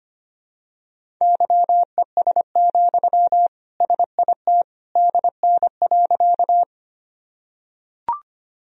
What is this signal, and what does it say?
Morse code, heard as a keyed tone, about 25 wpm, 700 Hz. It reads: YES, SIT DN.
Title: Sentences from Top 500 Words and Abbreviations 25wpm